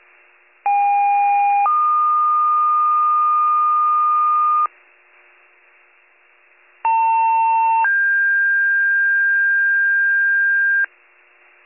Two-tone page, heard as a one-second tone, then a three-second tone, about 810 then 1220 Hz, about 900 then 1600 Hz. It is heard from 0.6 to 10.8 s.